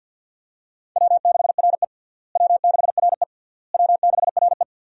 Morse code, heard as a keyed tone, about 51 wpm, 700 Hz. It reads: W6RE W6RE W6RE
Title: USA Call Signs - 4 Characters - Repeated 3x - 50wpm